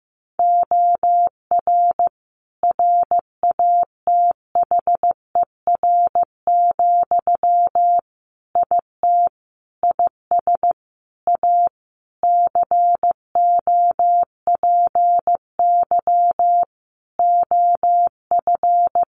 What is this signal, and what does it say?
Morse code, heard as a keyed tone, about 15 wpm, 700 Hz. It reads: OR RATHER, IT IS A COPY OF
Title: Sentences from Top 1400 Words 15wpm